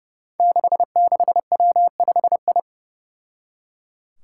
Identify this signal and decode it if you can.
Morse code, heard as a keyed tone, about 30 wpm, 700 Hz. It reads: 66W5I
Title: Five Challenging Letter-Number Combinations 30wpm